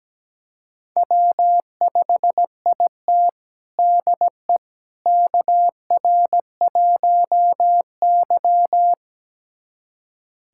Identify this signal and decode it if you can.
Morse code, heard as a keyed tone, about 17 wpm, 700 Hz. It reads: W5IT DE KR1Y
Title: USA Callsign DE Callsign - 4 Characters - 17wpm